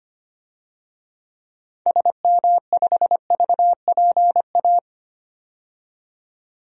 Morse code, heard as a keyed tone, about 25 wpm, 700 Hz. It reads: SM5VPA
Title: DXCC-100 Call Signs - 6 Characters - 25wpm